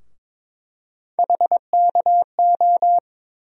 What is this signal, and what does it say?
Morse code, heard as a keyed tone, about 22 wpm, 700 Hz. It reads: HKO
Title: USA Call Sign Postfixes - 3 Letters 22wpm